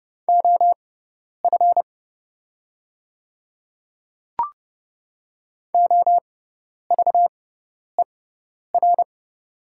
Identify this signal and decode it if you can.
Morse code, heard as a keyed tone, about 30 wpm, spaced as for 10 wpm, 700 Hz. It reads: OF OVER
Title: 6x Character Spacing - Top 100 Words - 30wpm